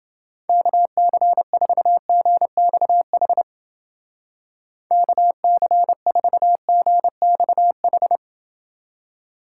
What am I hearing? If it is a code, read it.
Morse code, heard as a keyed tone, about 29 wpm, 700 Hz. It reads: KC4GXH KC4GXH